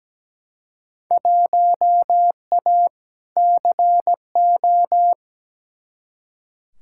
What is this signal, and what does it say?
Morse code, heard as a keyed tone, about 17 wpm, 700 Hz. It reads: 1A CO
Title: Field Day Exchanges - 17wpm